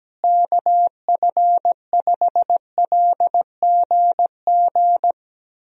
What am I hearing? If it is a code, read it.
Morse code, heard as a keyed tone, about 17 wpm, 700 Hz. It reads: KF5LGG
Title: USA Call Signs - 6 Characters - Repeated 2x - 17wpm